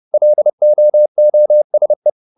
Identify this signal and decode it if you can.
Morse code, heard as a keyed tone, about 30 wpm, 600 Hz. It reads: LOOSE